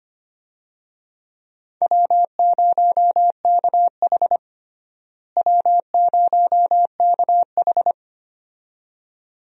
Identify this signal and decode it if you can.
Morse code, heard as a keyed tone, about 25 wpm, 700 Hz. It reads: W0KH W0KH
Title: USA Call Signs - 4 Characters - Repeated 2x - 25wpm